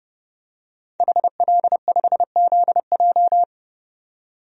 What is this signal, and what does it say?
Morse code, heard as a keyed tone, about 30 wpm, 700 Hz. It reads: HL5ZJ
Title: DXCC-100 Call Signs - 5 Characters - 30wpm